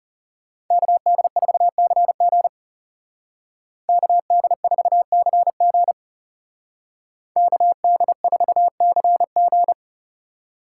Morse code, heard as a keyed tone, about 37 wpm, 700 Hz. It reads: KD4CG KD4CG KD4CG
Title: Speed Racing - USA 2x2 Call Signs 30wpm